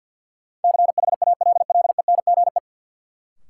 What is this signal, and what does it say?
Morse code, heard as a keyed tone, about 53 wpm, 700 Hz. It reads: CHALLENGE
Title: Sentences from Top 1000 Words - Review Part 7 - New Word Drill - 50wpm